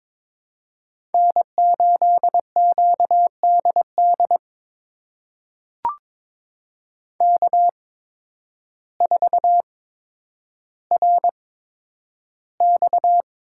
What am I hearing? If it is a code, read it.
Morse code, heard as a keyed tone, about 22 wpm, spaced as for 10 wpm, 700 Hz. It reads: N8QDD K4RX
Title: USA Call Signs - 5 Characters - 8x Character Spacing - 22wpm